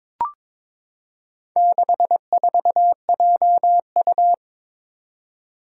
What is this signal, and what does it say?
Morse code, heard as a keyed tone, about 22 wpm, 700 Hz. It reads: 64JU
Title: Four Challenging Letter-Number Combinations 22wpm